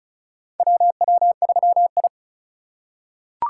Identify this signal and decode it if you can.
Morse code, heard as a keyed tone, about 36 wpm, 700 Hz. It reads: WW3I
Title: USA Call Signs - 2x Character Spacing - 35wpm